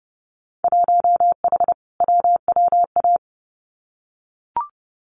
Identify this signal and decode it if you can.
Morse code, heard as a keyed tone, about 30 wpm, 700 Hz. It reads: 1H WWA
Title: Winter Field Day Exchanges 30wpm